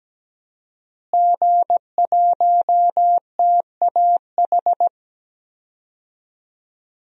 Morse code, heard as a keyed tone, about 17 wpm, 700 Hz. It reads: G1TAH
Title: DXCC-100 Call Signs - 5 Characters - Repeated 3x - 17wpm